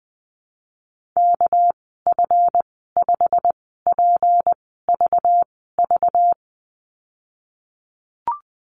Morse code, heard as a keyed tone, about 20 wpm, spaced as for 15 wpm, 700 Hz. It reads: KF5PVV
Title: 200 Lesson 32g - Letter k - Call Signs 20wpm